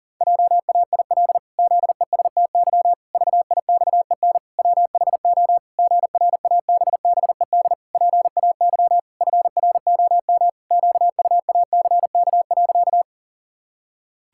Morse code, heard as a keyed tone, about 37 wpm, 700 Hz. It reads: JAIL ZESTY VIXEN WHO GRABBED PAY FROM QUACK.
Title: Pangram Challenge v2 - 40wpm